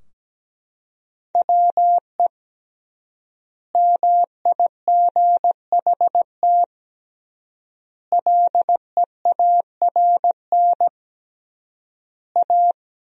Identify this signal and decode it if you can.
Morse code, heard as a keyed tone, about 17 wpm, 700 Hz. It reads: WE MIGHT LEARN A
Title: 3x Word-Spacing - Sentences from Top 300 Words - 17wpm